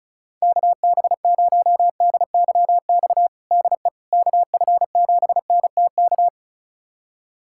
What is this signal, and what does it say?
Morse code, heard as a keyed tone, about 34 wpm, 700 Hz. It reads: KB0DYX DE KF7NTK
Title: USA Callsign DE Callsign - 6 Characters - 35wpm